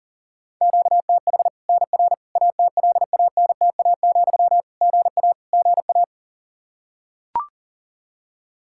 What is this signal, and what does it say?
Morse code, heard as a keyed tone, about 37 wpm, 700 Hz. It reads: QTH NR ATLANTA, GA GA
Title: K1USN SST QSO Practice 40wpm